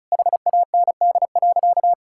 Morse code, heard as a keyed tone, about 34 wpm, 700 Hz. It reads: HAND.
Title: Sentences from Top 1600 Words 35wpm